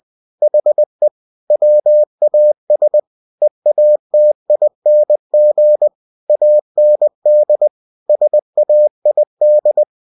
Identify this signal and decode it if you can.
Morse code, heard as a keyed tone, about 20 wpm, 600 Hz. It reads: HE WAS EATING AND SAID
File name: cw-books / winnie-the-pooh / winnie-20wpm-page-0033